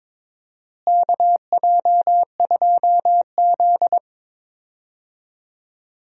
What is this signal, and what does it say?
Morse code, heard as a keyed tone, about 21 wpm, 700 Hz. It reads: KJ2Z
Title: USA Call Signs - 4 Characters - Repeated 3x - 22wpm